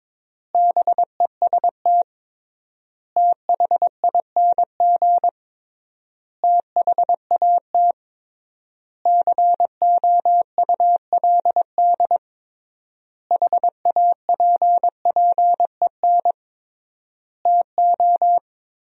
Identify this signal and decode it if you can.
Morse code, heard as a keyed tone, about 21 wpm, 700 Hz. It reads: BEST THING THAT COULD HAPPEN TO
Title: 3x Word-Spacing - Sentences from Top 400 Words - 22wpm